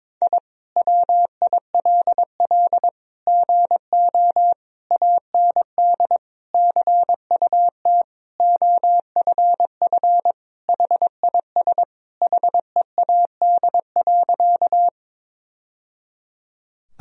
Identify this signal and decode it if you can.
Morse code, heard as a keyed tone, about 22 wpm, 700 Hz. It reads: I WILL GO AND CUT OFF HIS HEAD.
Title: Sentences from Top 1000 Words - Review Part 1 - 22wpm